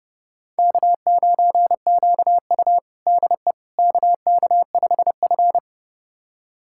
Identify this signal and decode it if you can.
Morse code, heard as a keyed tone, about 30 wpm, 700 Hz. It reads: K9QU DE KK5F
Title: USA Callsign DE Callsign - 4 Characters - 30wpm